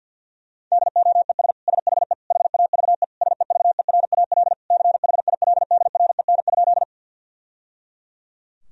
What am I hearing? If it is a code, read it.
Morse code, heard as a keyed tone, about 49 wpm, 700 Hz. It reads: DOES SHE HAVE SEVERAL CHILDREN?
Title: Sentences from Top 1000 Words - Review Part 2 - 50wpm